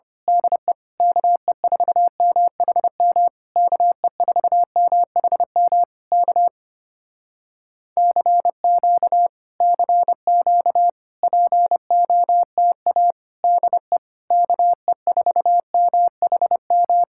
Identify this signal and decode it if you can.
Morse code, heard as a keyed tone, about 25 wpm, 700 Hz. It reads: DE KE4MHM KE4MHM K CQ CQ POTA DE KE4MHM
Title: Speed-Racing Common QSO Phrases - 25wpm